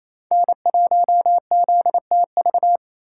Morse code, heard as a keyed tone, about 28 wpm, 700 Hz. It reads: N1ZTV
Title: USA Callsign DE Callsign 28wpm